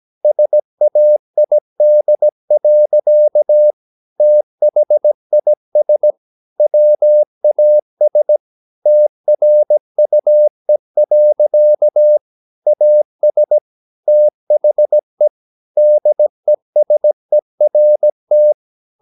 Morse code, heard as a keyed tone, about 17 wpm, 600 Hz. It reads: SAID. THIS WAS TRUE. AS THE DESERT
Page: Romanen Never af Ken Follet på engelsk.